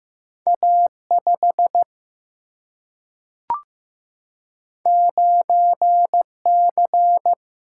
Morse code, heard as a keyed tone, about 15 wpm, 700 Hz. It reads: A5 9C
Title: Single-Letter Single-Number 15wpm